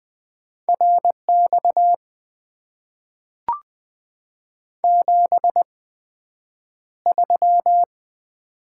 Morse code, heard as a keyed tone, about 20 wpm, spaced as for 9 wpm, 700 Hz. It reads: RX 73
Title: Top 90 QSO Elements - 8x Character Spacing - 20wpm